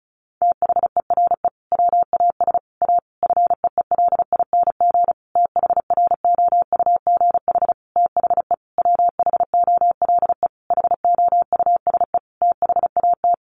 Morse code, heard as a keyed tone, about 34 wpm, 700 Hz. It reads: THERE WAS A FEELING THROUGH THE WHOLE HOUSE THAT
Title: Sentences from Top 1200 Words 35wpm